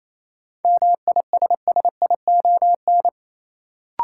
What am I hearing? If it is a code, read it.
Morse code, heard as a keyed tone, about 28 wpm, 700 Hz. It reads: MISSION
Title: Sentences from Top 1600 Words - New Word Drill - 28wpm